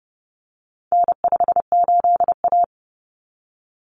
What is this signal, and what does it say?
Morse code, heard as a keyed tone, about 30 wpm, 700 Hz. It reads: N58A
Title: Common Cut Numbers - 4-Digit 30wpm